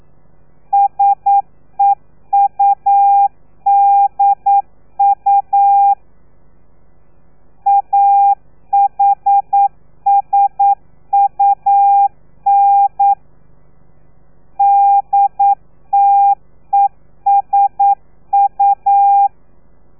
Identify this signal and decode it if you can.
Morse code, heard as a keyed tone, about 9 wpm, 800 Hz. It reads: SEUDU AHSUN DTESU
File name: telegrafia / i2tfj / Lez3 40